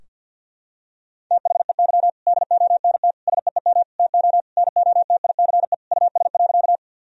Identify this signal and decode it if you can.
Morse code, heard as a keyed tone, about 50 wpm, 700 Hz. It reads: THEY DONT SEEM TO NOTICE US.
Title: Sentences from Top 1000 Words - Review Part 2 - 50wpm